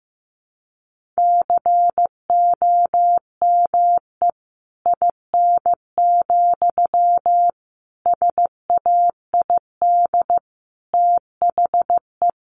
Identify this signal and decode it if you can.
Morse code, heard as a keyed tone, about 15 wpm, 700 Hz. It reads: COME IN, SAID THE
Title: Sentences from Top 1500 Words 15wpm